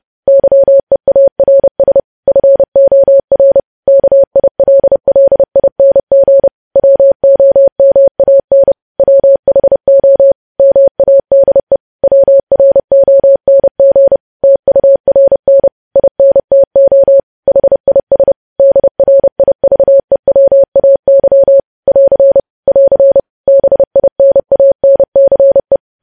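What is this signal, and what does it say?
Morse code, heard as a keyed tone, about 30 wpm, 550 Hz. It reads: YEARS FOR KILLING WOMAN WHO MADE WRONG TURN INTO HIS DRIVEWAY <AR> <AR> BINANCE
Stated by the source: Morse code news headlines sent at 30 words per minute.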